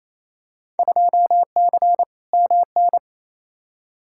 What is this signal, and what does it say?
Morse code, heard as a keyed tone, about 28 wpm, 700 Hz. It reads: 2C MN
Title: Field Day Exchanges Encore - 28wpm